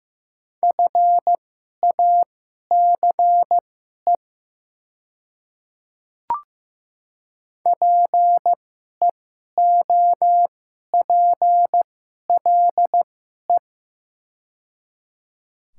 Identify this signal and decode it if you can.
Morse code, heard as a keyed tone, about 15 wpm, spaced as for 11 wpm, 700 Hz. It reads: FACE PEOPLE
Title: 2x Character Spacing - Top 300 Words - 15wpm